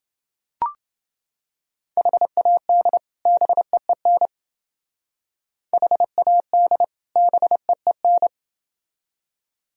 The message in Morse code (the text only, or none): HAD BEEN HAD BEEN